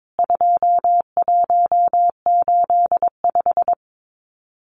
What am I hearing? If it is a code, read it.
Morse code, heard as a keyed tone, about 22 wpm, 700 Hz. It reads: 2185
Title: CWT Exchange 22wpm